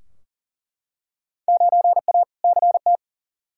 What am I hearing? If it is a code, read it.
Morse code, heard as a keyed tone, about 40 wpm, 700 Hz. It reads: 9A CT
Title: Field Day Exchanges Encore - 40wpm